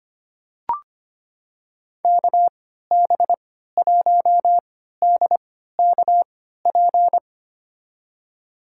Morse code, heard as a keyed tone, about 25 wpm, spaced as for 14 wpm, 700 Hz. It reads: KB1DKP
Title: USA Call Signs - 6 Characters - 3x Character Spacing - 25wpm